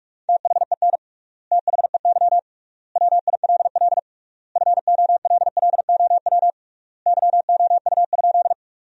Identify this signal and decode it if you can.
Morse code, heard as a keyed tone, about 46 wpm, 700 Hz. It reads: THEN THEY WILL FOLLOW YOU?
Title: 3x Word-Spacing - Sentences from Top 200 Words - 45wpm